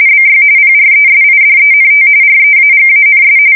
RTTY, about 45.45 baud. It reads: BY PRESIDENT REAGANS
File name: fsk170 / battleshipnj